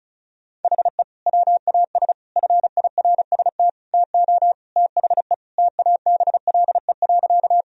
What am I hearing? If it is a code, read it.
Morse code, heard as a keyed tone, about 36 wpm, 700 Hz. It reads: HE WAS FIRST TO THE TABLE.
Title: Sentences from Top 1000 Words - Review Part 2 - 35wpm